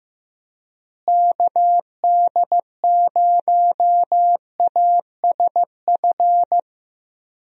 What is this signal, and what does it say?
Morse code, heard as a keyed tone, about 15 wpm, 700 Hz. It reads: KD0ASF
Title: USA Call Signs - 6 Characters - Repeated 3x - 15wpm